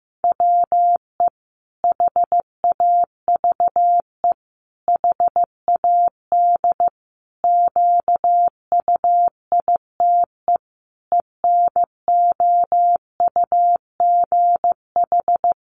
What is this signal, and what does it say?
Morse code, heard as a keyed tone, about 15 wpm, 700 Hz. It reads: WE HAVE HAD QUITE ENOUGH
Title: Sentences from Top 700 Words 15wpm